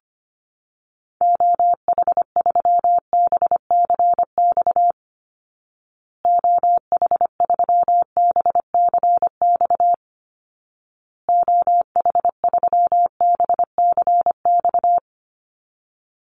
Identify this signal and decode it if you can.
Morse code, heard as a keyed tone, about 25 wpm, 700 Hz. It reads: OH3BCX OH3BCX OH3BCX
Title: Call Signs Repeated 2x - 25wpm